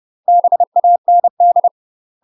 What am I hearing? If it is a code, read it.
Morse code, heard as a keyed tone, about 30 wpm, 700 Hz. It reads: BAND
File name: UK MASTER QSOs TRAINING / 30WPM / UK MASTER QSOs 0011